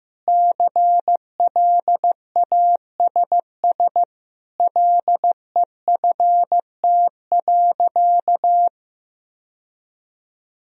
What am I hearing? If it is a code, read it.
Morse code, heard as a keyed tone, about 15 wpm, 700 Hz. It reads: CLASS LEFT.
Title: Sentences from Top 1000 Words - Review Part 2 - 15wpm